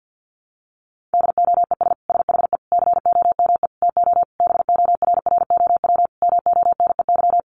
Morse code, heard as a keyed tone, about 50 wpm, 700 Hz. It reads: DOES SHE COME TO BORROW MONEY
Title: Sentences from Top 1500 Words 50wpm